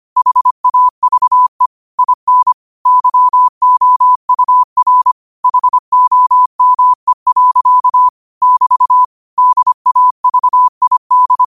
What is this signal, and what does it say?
Morse code, heard as a keyed tone, about 25 wpm, 1000 Hz. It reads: HAVE IN YOUR HOME. <BT> DAVID
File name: qotdfiles / 25wpm